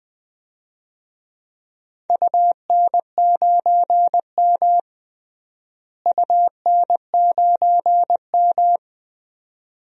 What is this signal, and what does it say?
Morse code, heard as a keyed tone, about 20 wpm, 700 Hz. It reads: UN9M UN9M
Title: DXCC-100 Call Signs - 4 Characters - Repeated 2x - 20wpm